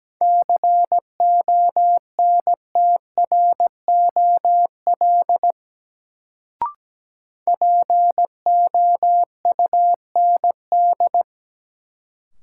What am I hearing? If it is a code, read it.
Morse code, heard as a keyed tone, about 17 wpm, 700 Hz. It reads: CONTROL POUND
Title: Sentences from Top 1000 Words - Review Part 2 - New Word Drill - 17wpm